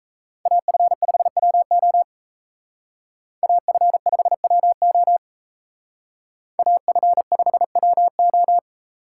Morse code, heard as a keyed tone, about 33 wpm, 700 Hz. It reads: AF5WO AF5WO AF5WO
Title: Speed Racing - USA 2x2 Call Signs 28wpm